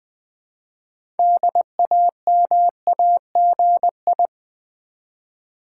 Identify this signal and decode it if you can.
Morse code, heard as a keyed tone, about 20 wpm, 700 Hz. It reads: DAMAGI
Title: Build A Word - 2 Letters At A Time - 8-Letter Words - 20wpm